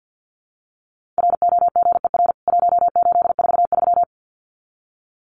Morse code, heard as a keyed tone, about 48 wpm, 700 Hz. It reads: ROGER 1843
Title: CWT Exchange 50wpm